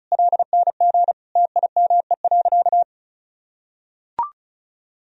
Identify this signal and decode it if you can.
Morse code, heard as a keyed tone, about 34 wpm, 700 Hz. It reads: LNG TIME.
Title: Sentences from Top 400 Words and Abbreviations 35wpm